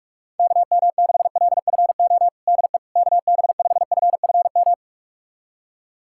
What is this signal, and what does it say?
Morse code, heard as a keyed tone, about 46 wpm, 700 Hz. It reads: KM6LFO DE KB5FFM
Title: USA Callsign DE Callsign - 6 Characters - 45wpm